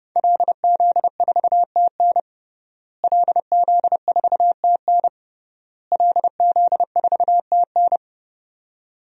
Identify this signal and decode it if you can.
Morse code, heard as a keyed tone, about 30 wpm, 700 Hz. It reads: LZ4TN LZ4TN LZ4TN
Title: DXCC-100 Call Signs - 5 Characters - Repeated 3x - 30wpm